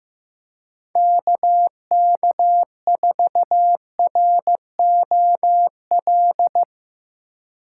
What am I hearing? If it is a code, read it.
Morse code, heard as a keyed tone, about 17 wpm, 700 Hz. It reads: KK4ROL
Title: Speed Racing - USA 2x3 Call Signs 15wpm